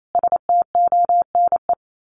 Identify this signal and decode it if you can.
Morse code, heard as a keyed tone, about 28 wpm, 700 Hz. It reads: STONE
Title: A Sound Of Thunder 28wpm